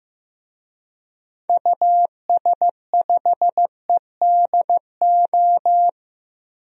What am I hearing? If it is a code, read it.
Morse code, heard as a keyed tone, about 15 wpm, 700 Hz. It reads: US5EDO
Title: DXCC-100 Call Signs - 6 Characters - Repeated 2x - 15wpm